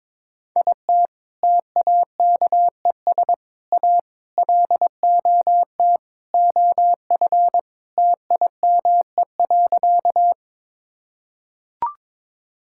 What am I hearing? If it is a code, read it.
Morse code, heard as a keyed tone, about 22 wpm, 700 Hz. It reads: IT TAKES A LOT OF TIME.
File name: practice / 22wpm / Sentences with 4-Word Phrases - 22wpm